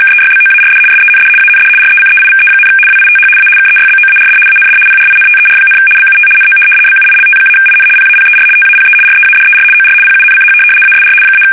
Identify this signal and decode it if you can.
RTTY, about 75 baud. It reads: KEYBOARD SEND AND RECEIVE, WITH A KEYBOARD AND PRINTER, BUT NO BUILT-IN PAPER TAPE READER OR PUNCH ASR -
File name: fsk850-75B / teletype-corp